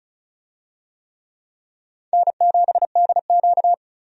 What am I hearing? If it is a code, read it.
Morse code, heard as a keyed tone, about 36 wpm, 700 Hz. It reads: N7DQ
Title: USA Call Signs - 4 Characters - 35wpm